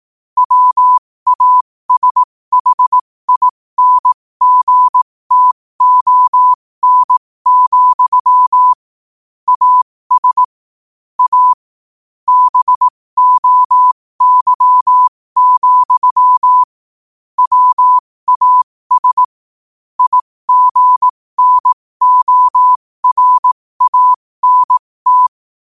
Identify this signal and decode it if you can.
Morse code, heard as a keyed tone, about 18 wpm, spaced as for 15 wpm, 1000 Hz. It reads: WASHINGTON, AS A BOY, WAS IGNORANT
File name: qotdfiles / 15wpm